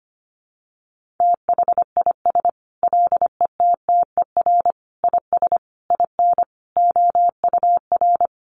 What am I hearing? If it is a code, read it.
Morse code, heard as a keyed tone, about 25 wpm, 700 Hz. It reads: THIS LETTER IS IN OUR
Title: Sentences from Top 1400 Words 25wpm